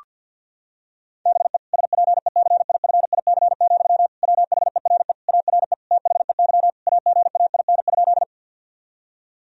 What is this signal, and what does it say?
Morse code, heard as a keyed tone, about 50 wpm, 700 Hz. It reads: BE SPECIFIC, WHERE ARE THEY AGAIN?
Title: Sentences from Top 1000 Words - Review Part 3 - 50wpm